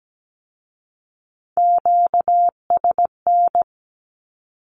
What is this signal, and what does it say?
Morse code, heard as a keyed tone, about 17 wpm, 700 Hz. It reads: QSN
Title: Q-Codes with Spoken Meaning - 17wpm